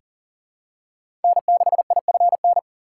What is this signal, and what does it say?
Morse code, heard as a keyed tone, about 40 wpm, 700 Hz. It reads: N6IFN
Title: USA Call Signs - 5 Characters - 2x Character Spacing - 40wpm